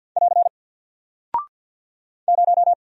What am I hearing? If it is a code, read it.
Morse code, heard as a keyed tone, about 48 wpm, 700 Hz. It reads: <AR> 0
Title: Single Character (Letter:Number:Punctuation:Prosign) 50wpm